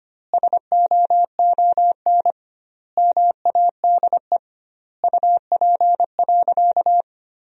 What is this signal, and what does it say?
Morse code, heard as a keyed tone, about 25 wpm, 700 Hz. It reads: SOON MADE UP.
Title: 2x Word-Spacing - Sentences from Top 500 Words - 25wpm